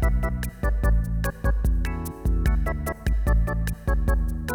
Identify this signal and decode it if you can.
There is no code.